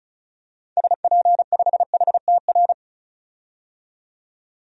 Morse code, heard as a keyed tone, about 34 wpm, 700 Hz. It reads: SP5HTR
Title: DXCC-100 Call Signs - 6 Characters - 35wpm